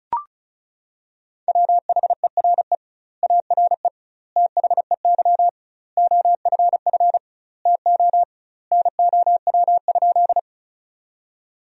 Morse code, heard as a keyed tone, about 36 wpm, 700 Hz. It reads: WHERE ARE THEY OFF TO NOW?
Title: 2x Word-Spacing - Sentences from Top 200 Words - 35wpm